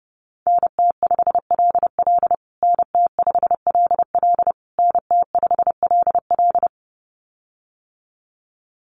Morse code, heard as a keyed tone, about 28 wpm, 700 Hz. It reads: NT5LL NT5LL NT5LL
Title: USA Call Signs Repeated 3x 30wpm